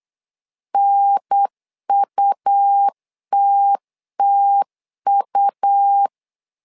Morse code, heard as a keyed tone, about 8 wpm, 800 Hz. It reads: NUTTU
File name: LEZIONI / 3 MWRO / 3 M 8wpm 2300